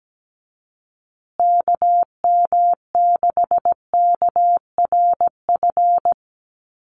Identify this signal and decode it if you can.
Morse code, heard as a keyed tone, about 17 wpm, 700 Hz. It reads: KM6KRF
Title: USA Call Signs Repeated 3x 17wpm